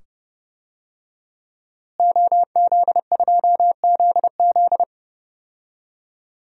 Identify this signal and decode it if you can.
Morse code, heard as a keyed tone, about 30 wpm, 700 Hz. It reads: OZ2ZZ
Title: DXCC-100 Call Signs - 5 Characters - Repeated 2x - 30wpm